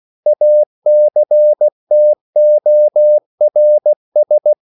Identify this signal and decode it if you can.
Morse code, heard as a keyed tone, about 16 wpm, 600 Hz. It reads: ACTORS
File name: CW / Practice / EnglishQuotations0416